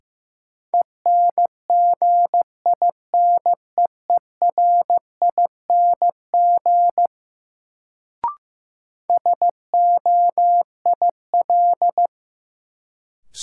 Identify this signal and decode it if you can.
Morse code, heard as a keyed tone, about 15 wpm, 700 Hz. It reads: ENGINEERING SOIL
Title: Sentences from Top 1500 Words - New Word Drill Alt - 15wpm